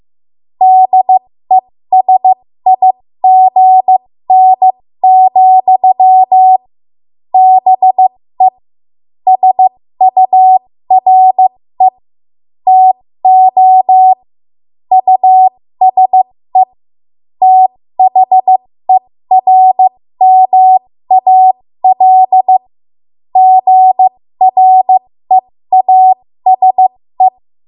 Morse code, heard as a keyed tone, about 15 wpm, spaced as for 13 wpm, 750 Hz. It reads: DESIGN, BE SURE TO USE THERMAL GREASE
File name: Morse / 13WPM